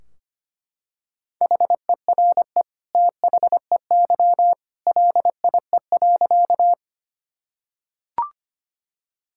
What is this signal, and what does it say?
Morse code, heard as a keyed tone, about 25 wpm, 700 Hz. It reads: HERE THEY LIE.
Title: Sentences from Top 1600 Words 25wpm